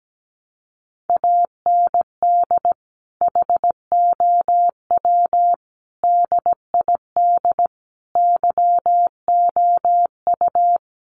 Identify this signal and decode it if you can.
Morse code, heard as a keyed tone, about 17 wpm, 700 Hz. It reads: AND HOW DID YOU
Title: Sentences from Top 1100 Words 17wpm